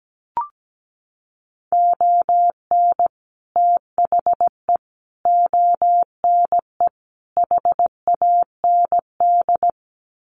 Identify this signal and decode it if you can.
Morse code, heard as a keyed tone, about 17 wpm, 700 Hz. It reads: ON THE ONE HAND
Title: Sets of 4 Words 17wpm